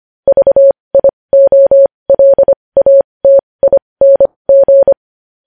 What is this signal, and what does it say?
Morse code, heard as a keyed tone, about 25 wpm, spaced as for 20 wpm, 550 Hz. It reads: VIOLATING
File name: podcast / News headlines-1x-25x20